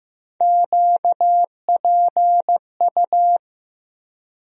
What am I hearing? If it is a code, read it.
Morse code, heard as a keyed tone, about 15 wpm, 700 Hz. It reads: QPU
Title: USA Call Sign Postfixes - 3 Letters 15wpm